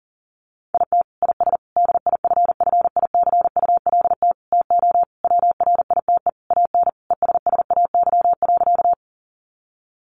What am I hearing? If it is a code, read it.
Morse code, heard as a keyed tone, about 40 wpm, 700 Hz. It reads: IT IS DIFFICULT TO WRITE AN ESSAY.
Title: Sentences from Top 1500 Words 40wpm